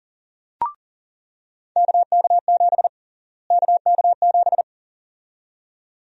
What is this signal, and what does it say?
Morse code, heard as a keyed tone, about 40 wpm, 700 Hz. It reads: KK7 KK7
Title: USA Call Sign Prefixes - Repeated 2x - 40wpm